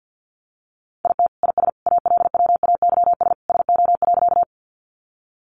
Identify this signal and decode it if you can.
Morse code, heard as a keyed tone, about 48 wpm, 700 Hz. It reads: IT IS ALWAYS SO.
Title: Sentences from Top 300 Words 50wpm